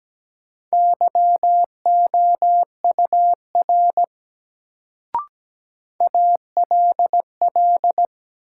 Morse code, heard as a keyed tone, about 17 wpm, 700 Hz. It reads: YOUR ALL
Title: Sentences from Top 100 Words - New Word Drill - 17wpm